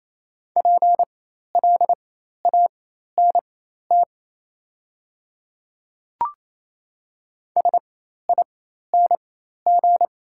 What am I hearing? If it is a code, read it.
Morse code, heard as a keyed tone, about 28 wpm, spaced as for 13 wpm, 690 Hz. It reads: PLANT SING
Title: 4x Character Spacing - Top 500 Words - 28wpm